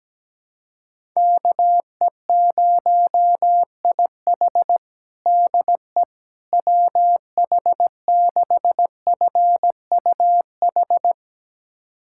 Morse code, heard as a keyed tone, about 17 wpm, 700 Hz. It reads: KE0IH DE WH6FUH
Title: USA Callsign DE Callsign 17wpm